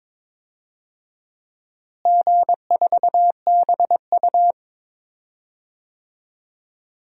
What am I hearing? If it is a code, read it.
Morse code, heard as a keyed tone, about 22 wpm, 700 Hz. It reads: G4BU
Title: DXCC-100 Call Signs - 4 Characters - Repeated 3x - 22wpm